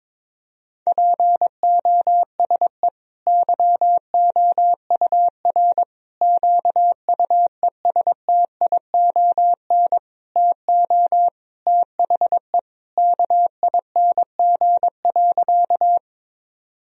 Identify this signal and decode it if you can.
Morse code, heard as a keyed tone, about 21 wpm, 700 Hz. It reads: POSE YOUR QUESTION TO THE KING.
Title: Sentences from Top 1000 Words - Review Part 2 - 22wpm